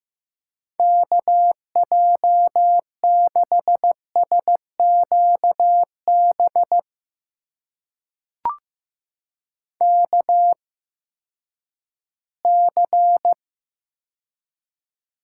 Morse code, heard as a keyed tone, about 15 wpm, spaced as for 10 wpm, 700 Hz. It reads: KJ6SQB KC
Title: USA Call Signs - 6 Characters - 8x Character Spacing - 15wpm